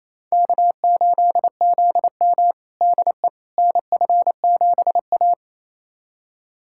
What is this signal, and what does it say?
Morse code, heard as a keyed tone, about 27 wpm, 700 Hz. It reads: K8ZM DE NF7A
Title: USA Callsign DE Callsign - 4 Characters - 28wpm